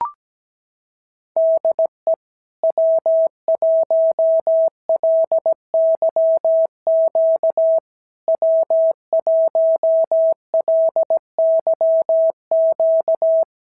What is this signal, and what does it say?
Morse code, heard as a keyed tone, about 17 wpm, 650 Hz. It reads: DE W1LYQ W1LYQ
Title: SOTA QSO Practice 17wpm